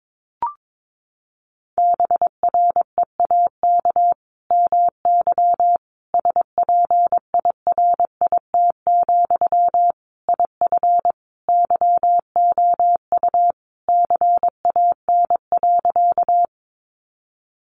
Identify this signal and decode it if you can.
Morse code, heard as a keyed tone, about 22 wpm, 700 Hz. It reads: BREAK MY SPIRIT, IF YOU CAN.
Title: Sentences from Top 1400 Words 22wpm